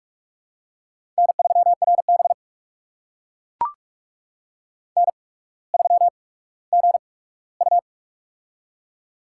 Morse code, heard as a keyed tone, about 44 wpm, spaced as for 19 wpm, 700 Hz. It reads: N3RB N3GU
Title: USA Call Signs - 4 Characters - 8x Character Spacing - 45wpm